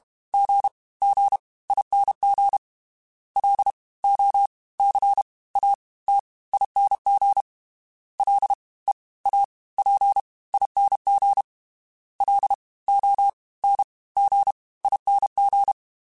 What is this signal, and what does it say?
Morse code, heard as a keyed tone, about 32 wpm, 800 Hz. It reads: G G ING L O C A T ING L E A P ING L O N G ING
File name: Special INGs / Special INGs-mp3s / Fast ing @30